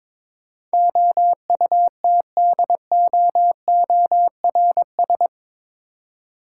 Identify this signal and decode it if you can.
Morse code, heard as a keyed tone, about 22 wpm, 700 Hz. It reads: OUTDOORS
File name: practice / 22wpm / Eight Letter Words - 22wpm